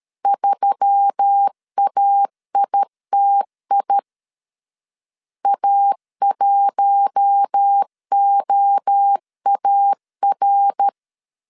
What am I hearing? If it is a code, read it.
Morse code, heard as a keyed tone, about 13 wpm, 800 Hz. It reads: 3AITI A1OAR
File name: CW / DET / 1 6 1500C S A 12wpm 3440